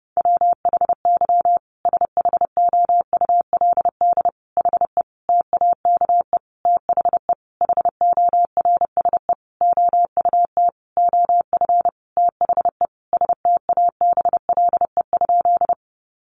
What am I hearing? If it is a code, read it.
Morse code, heard as a keyed tone, about 28 wpm, 700 Hz. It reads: WHY SHOULD HE TAKE THE HORSE OUT OF THE STABLE?
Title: Sentences from Top 1500 Words 30wpm